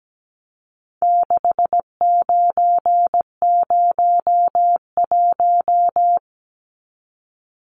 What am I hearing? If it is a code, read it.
Morse code, heard as a keyed tone, about 17 wpm, 700 Hz. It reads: 6901
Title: Numbers - 4-Digit 17wpm